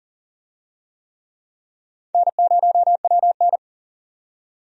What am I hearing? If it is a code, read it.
Morse code, heard as a keyed tone, about 37 wpm, 700 Hz. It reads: N0WN
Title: USA Call Signs - 4 Characters - 40wpm